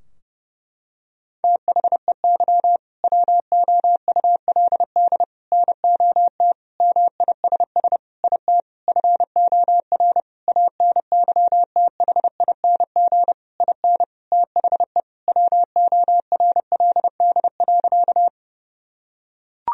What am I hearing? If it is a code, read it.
Morse code, heard as a keyed tone, about 32 wpm, 700 Hz. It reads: THEY WOULD NOT MISS IT FOR ANYTHING IN THE WORLD.
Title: Sentences from Top 1000 Words - Review Part 3 - 30wpm